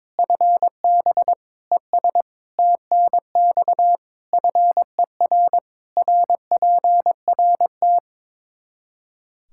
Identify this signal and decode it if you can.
Morse code, heard as a keyed tone, about 22 wpm, 700 Hz. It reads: FB ES TNX FER RPRT
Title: Practice QSO - Exchange 2 - Info - 22wpm